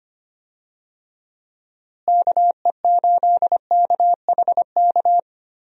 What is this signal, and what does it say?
Morse code, heard as a keyed tone, about 25 wpm, 700 Hz. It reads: KE8KHK
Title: USA Call Signs - 6 Characters - Repeated 3x - 25wpm